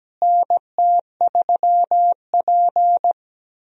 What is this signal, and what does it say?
Morse code, heard as a keyed tone, about 17 wpm, 700 Hz. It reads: NT3P
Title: USA Call Signs - 4 Characters - Repeated 3x - 17wpm